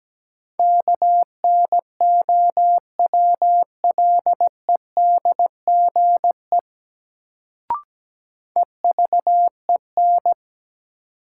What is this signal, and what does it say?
Morse code, heard as a keyed tone, about 17 wpm, 700 Hz. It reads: KNOWLEDGE EVEN
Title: Sentences from Top 600 Words - New Word Drill - 17wpm